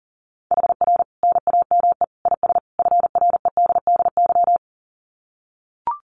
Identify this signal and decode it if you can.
Morse code, heard as a keyed tone, about 37 wpm, 700 Hz. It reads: HR NAME IS FREDDY
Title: Common QSO Phrases 40wpm